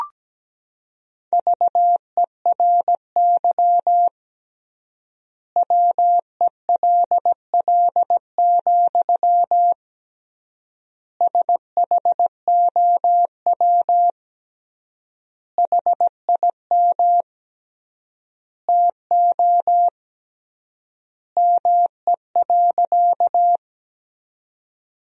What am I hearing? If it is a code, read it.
Morse code, heard as a keyed tone, about 17 wpm, 700 Hz. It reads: VERY WELL, SHOW HIM TO ME.
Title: 3x Word-Spacing - Sentences from Top 200 Words - 17wpm